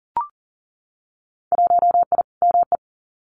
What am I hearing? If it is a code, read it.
Morse code, heard as a keyed tone, about 42 wpm, 700 Hz. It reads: 1I ME
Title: Winter Field Day Exchanges 40wpm